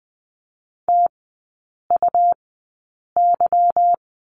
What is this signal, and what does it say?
Morse code, heard as a keyed tone, about 20 wpm, spaced as for 8 wpm, 700 Hz. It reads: TUY
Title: 152 Lesson 25e - Letter p - Three Characters 20wpm